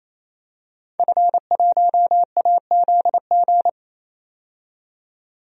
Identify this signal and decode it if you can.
Morse code, heard as a keyed tone, about 28 wpm, 700 Hz. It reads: F1AZG
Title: DXCC-100 Call Signs - 5 Characters - 28wpm